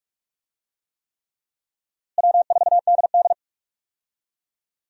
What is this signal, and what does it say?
Morse code, heard as a keyed tone, about 46 wpm, 700 Hz. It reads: W4DD
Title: USA Call Signs - 4 Characters - Repeated 3x - 45wpm